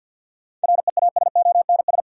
Morse code, heard as a keyed tone, about 51 wpm, 700 Hz. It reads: REASONS
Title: Sets of 4 Words - New Word Drill - 50wpm